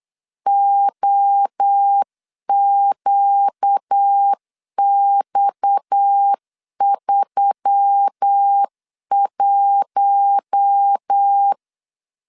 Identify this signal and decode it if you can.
Morse code, heard as a keyed tone, about 8 wpm, 800 Hz. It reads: OQX31
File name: LEZIONI / 7 GF9Z / 7 G 8wpm 3118